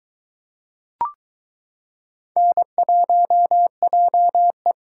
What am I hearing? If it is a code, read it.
Morse code, heard as a keyed tone, about 22 wpm, 700 Hz. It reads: N1JE